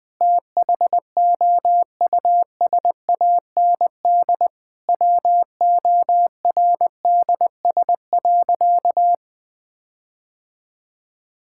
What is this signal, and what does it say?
Morse code, heard as a keyed tone, about 20 wpm, 700 Hz. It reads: THOUSAND WORDS.
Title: Sentences with 7-Word Phrases - 20wpm